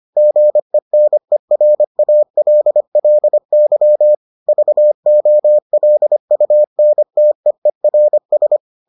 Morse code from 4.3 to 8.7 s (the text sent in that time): VOLUNTEERS